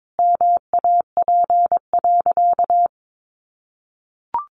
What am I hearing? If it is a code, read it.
Morse code, heard as a keyed tone, about 22 wpm, 700 Hz. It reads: MAP.
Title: Additional Easy Phrases 22wpm